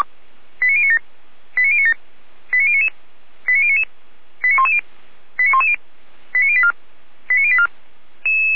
Five-tone selective call (ZVEI), heard as quick runs of five tones, tones of about 70 ms. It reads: 89097 89097 89099 89099 89110 89110 89084 89084